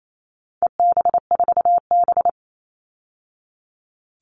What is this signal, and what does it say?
Morse code, heard as a keyed tone, about 28 wpm, 700 Hz. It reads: EB4B
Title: All Cut Numbers - 4-Digit 28wpm